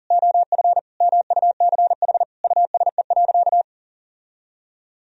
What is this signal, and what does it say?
Morse code, heard as a keyed tone, about 40 wpm, 700 Hz. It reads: OF MUCH USE.
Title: Sentences from Top 1000 Words - Review Part 3 - 40wpm